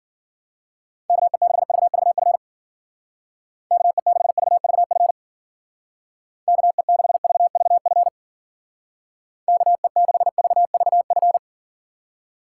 Morse code, heard as a keyed tone, about 48 wpm, 700 Hz. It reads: KE6VVF KE6VVF KE6VVF KE6VVF
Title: Speed Racing - USA 2x3 Call Signs 40wpm